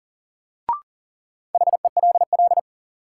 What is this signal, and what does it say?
Morse code, heard as a keyed tone, about 40 wpm, 700 Hz. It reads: HELL
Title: Sets of 5 Words Encore - New Word Drill - 40wpm